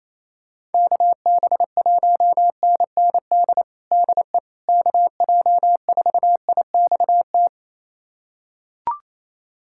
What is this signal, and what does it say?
Morse code, heard as a keyed tone, about 28 wpm, 700 Hz. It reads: KB1NND DE KJ4IXT
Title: USA Callsign DE Callsign - 6 Characters - 28wpm